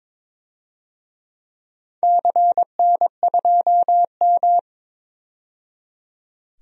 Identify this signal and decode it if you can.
Morse code, heard as a keyed tone, about 22 wpm, 700 Hz. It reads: CN2M
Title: DXCC-100 Call Signs - 4 Characters - 22wpm